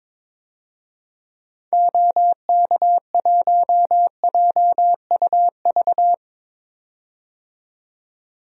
Morse code, heard as a keyed tone, about 22 wpm, 700 Hz. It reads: OK1JUV
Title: DXCC-100 Call Signs - 6 Characters - 22wpm